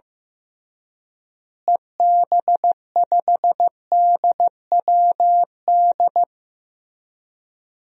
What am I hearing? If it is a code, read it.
Morse code, heard as a keyed tone, about 15 wpm, 700 Hz. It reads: EB5DWD
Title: DXCC-100 Call Signs - 6 Characters - Repeated 3x - 15wpm